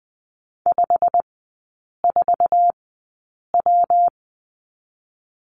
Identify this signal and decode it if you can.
Morse code, heard as a keyed tone, about 19 wpm, spaced as for 8 wpm, 700 Hz. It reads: 54W
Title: 127 Lesson 21e - Number 6 - Three Characters 20wpm